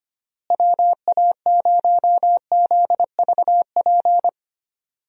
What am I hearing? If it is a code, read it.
Morse code, heard as a keyed tone, about 25 wpm, 700 Hz. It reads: WA0ZVP
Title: USA Call Signs - 6 Characters - Repeated 2x - 25wpm